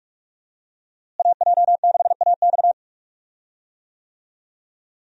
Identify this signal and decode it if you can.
Morse code, heard as a keyed tone, about 46 wpm, 700 Hz. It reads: AJ6AX
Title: USA Call Signs - 5 Characters - Repeated 3x - 45wpm